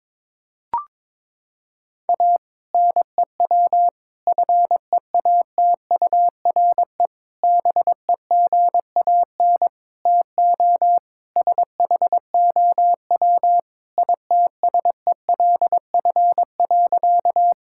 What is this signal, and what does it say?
Morse code, heard as a keyed tone, about 22 wpm, 700 Hz. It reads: A NEW FEATURE BEGAN TO SHOW ITSELF.
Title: Sentences from Top 1000 Words - Review Part 5 - 22wpm